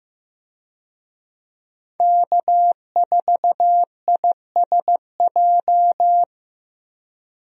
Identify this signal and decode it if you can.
Morse code, heard as a keyed tone, about 15 wpm, 700 Hz. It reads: K4ISJ
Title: USA Call Signs - 5 Characters - Repeated 2x - 15wpm